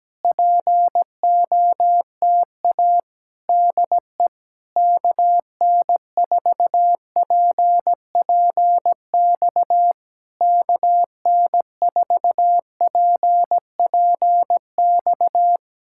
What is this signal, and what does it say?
Morse code, heard as a keyed tone, about 17 wpm, 700 Hz. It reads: POTA DE KN4PPX KN4PPX
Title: Speed-Racing Common QSO Phrases - 17wpm